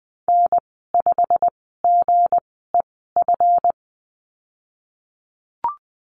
Morse code, heard as a keyed tone, about 19 wpm, spaced as for 14 wpm, 700 Hz. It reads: N5GEF
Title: 161 Lesson 26g - Letter g - Call Signs 20wpm